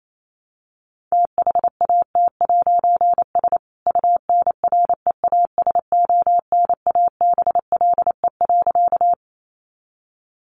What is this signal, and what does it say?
Morse code, heard as a keyed tone, about 28 wpm, 700 Hz. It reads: THAT'S UNREASONABLE.
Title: A Sound Of Thunder 28wpm